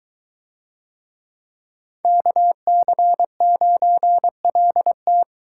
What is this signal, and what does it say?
Morse code, heard as a keyed tone, about 22 wpm, 700 Hz. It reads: KC9LT